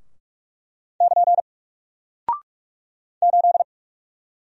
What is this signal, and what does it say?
Morse code, heard as a keyed tone, about 46 wpm, 700 Hz. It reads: <KN> 8
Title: Single Character (Letter:Number:Punctuation:Prosign) Alt 45wpm